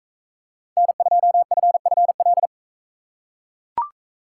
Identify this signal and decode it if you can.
Morse code, heard as a keyed tone, about 41 wpm, 700 Hz. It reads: N2FFL